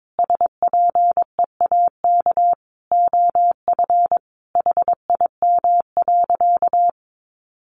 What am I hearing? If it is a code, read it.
Morse code, heard as a keyed tone, about 22 wpm, 700 Hz. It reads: SPEAK OF HIM.
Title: Sentences from Top 1100 Words 22wpm